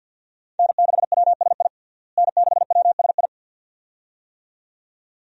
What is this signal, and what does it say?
Morse code, heard as a keyed tone, about 51 wpm, 700 Hz. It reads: N6WSI N6WSI
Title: USA Call Signs - 5 Characters - Repeated 2x - 50wpm